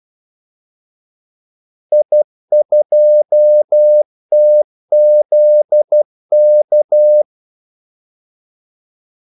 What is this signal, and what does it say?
Morse code, heard as a keyed tone, about 12 wpm, 600 Hz. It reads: I2TZK
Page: Kaldesignaler 12 wpm